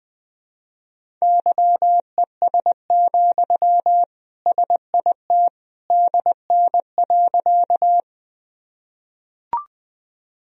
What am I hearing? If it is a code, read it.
Morse code, heard as a keyed tone, about 21 wpm, 700 Hz. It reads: YES, SIT DN.
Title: Sentences from Top 500 Words and Abbreviations 20wpm